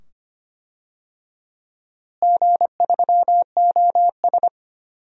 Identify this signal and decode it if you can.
Morse code, heard as a keyed tone, about 25 wpm, 700 Hz. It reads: G3OS
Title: DXCC-100 Call Signs - 4 Characters - Repeated 2x - 25wpm